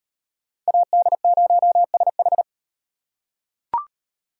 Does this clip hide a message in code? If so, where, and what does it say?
Yes: Morse code from 0.7 to 2.4 s, heard as a keyed tone, about 38 wpm, 700 Hz. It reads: AD0SH